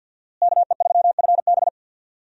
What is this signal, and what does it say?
Morse code, heard as a keyed tone, about 50 wpm, 700 Hz. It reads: KE3FB
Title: USA Call Signs - 5 Characters - Repeated 3x - 50wpm